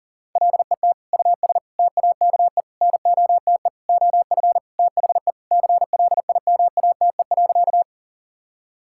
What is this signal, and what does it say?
Morse code, heard as a keyed tone, about 40 wpm, 700 Hz. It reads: LET US TAKE NOTE OF THE CLIMATE.
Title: Sentences from Top 1600 Words 40wpm